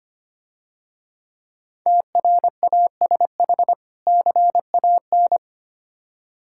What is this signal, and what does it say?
Morse code, heard as a keyed tone, about 26 wpm, 700 Hz. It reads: TRASH CAN
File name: Easy Words - Speed-Racing with Hints - 25wpm